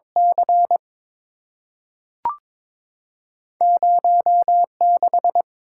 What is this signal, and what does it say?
Morse code, heard as a keyed tone, about 22 wpm, 700 Hz. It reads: C 0 6
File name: practice / 22wpm / Two Letter-Number Combinations - 22wpm